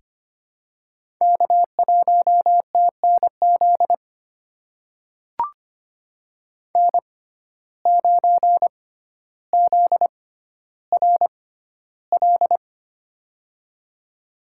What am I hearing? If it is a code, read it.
Morse code, heard as a keyed tone, about 25 wpm, spaced as for 13 wpm, 700 Hz. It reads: K1TNZ N9ZRL
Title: USA Call Signs - 5 Characters - 6x Character Spacing - 25wpm